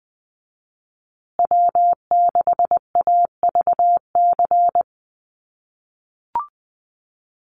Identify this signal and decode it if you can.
Morse code, heard as a keyed tone, about 20 wpm, 700 Hz. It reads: W6AVC
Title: USA Call Signs Repeated 3x 20wpm